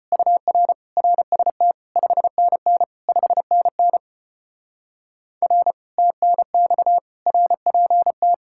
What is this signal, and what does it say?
Morse code, heard as a keyed tone, about 33 wpm, 700 Hz. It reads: UR RST 5NN 5NN R TNX RPT